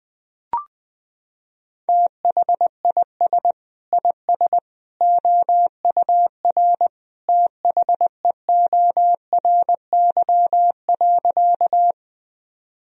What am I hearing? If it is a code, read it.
Morse code, heard as a keyed tone, about 21 wpm, 700 Hz. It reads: THIS IS OUR THEORY.
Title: Sentences from Top 1000 Words - Review Part 3 - 20wpm